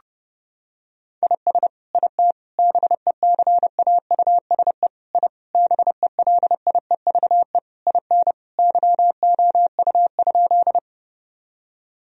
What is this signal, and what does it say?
Morse code, heard as a keyed tone, about 30 wpm, 700 Hz. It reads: IS IT BECAUSE I BELIEVE IN YOU?
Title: Sentences from Top 1000 Words - Review Part 3 - 30wpm